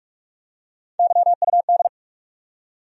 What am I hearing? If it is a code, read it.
Morse code, heard as a keyed tone, about 46 wpm, 700 Hz. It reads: YUD
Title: USA Call Sign Postfixes - 3 Letters 45wpm